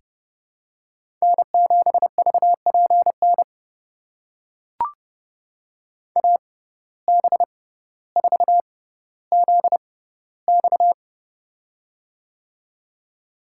Morse code, heard as a keyed tone, about 29 wpm, spaced as for 15 wpm, 700 Hz. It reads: N7VPN AB4ZX
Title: USA Call Signs - 5 Characters - 6x Character Spacing - 30wpm